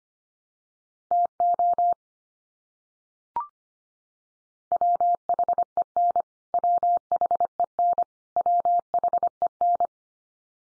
Morse code, heard as a keyed tone, about 25 wpm, 700 Hz. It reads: TO WHEN WHEN WHEN
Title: Top 100 Words 25wpm x3